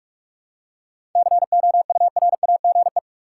Morse code, heard as a keyed tone, about 46 wpm, 700 Hz. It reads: COURAGE
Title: Seven Letter Words - 45wpm